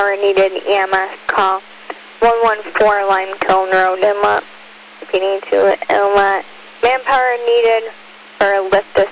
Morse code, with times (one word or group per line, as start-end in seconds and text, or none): none